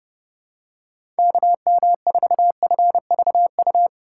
Morse code, heard as a keyed tone, about 30 wpm, 700 Hz. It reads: KM4FVU
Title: USA Call Signs - 6 Characters - 6x Character Spacing - 30wpm